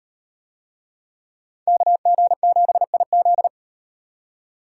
Morse code, heard as a keyed tone, about 37 wpm, 700 Hz. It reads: KG7IZ